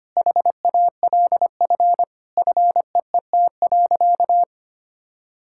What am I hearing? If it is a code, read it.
Morse code, heard as a keyed tone, about 25 wpm, 700 Hz. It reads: HALF FEET.
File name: practice / 25wpm / Sentences with 3-Word Phrases - 25wpm